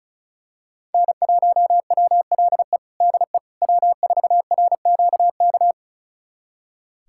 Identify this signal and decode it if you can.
Morse code, heard as a keyed tone, about 34 wpm, 700 Hz. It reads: N1WLE DE W4RQK
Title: USA Callsign DE Callsign - 5 Characters - 35wpm